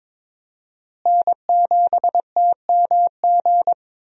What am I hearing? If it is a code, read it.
Morse code, heard as a keyed tone, about 22 wpm, 700 Hz. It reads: N7TMG
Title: USA Call Signs - 5 Characters - Repeated 3x - 22wpm